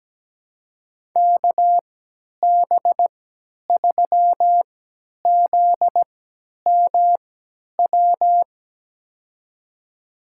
Morse code, heard as a keyed tone, about 17 wpm, spaced as for 10 wpm, 700 Hz. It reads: KB3ZMW
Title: USA Call Signs - 6 Characters - 3x Character Spacing - 17wpm